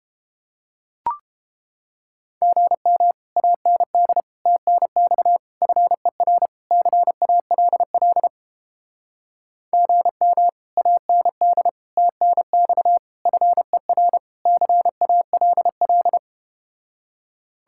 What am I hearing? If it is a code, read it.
Morse code, heard as a keyed tone, about 31 wpm, 700 Hz. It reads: GM AND TNX FER CALL GM AND TNX FER CALL